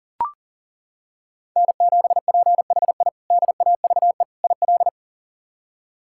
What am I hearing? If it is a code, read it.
Morse code, heard as a keyed tone, about 37 wpm, 700 Hz. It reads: N7PHI DAVE IL
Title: K1USN SST QSO Practice - 40wpm